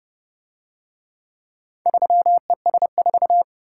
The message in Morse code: 3ES4

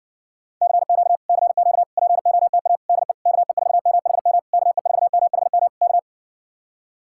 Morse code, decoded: CQ CQ POTA DE KE4MHM KE4MHM K

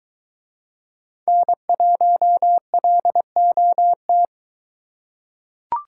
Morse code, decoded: N1LOT

